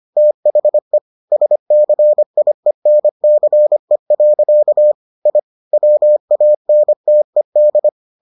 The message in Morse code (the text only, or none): THE SCIENCE. I WANTED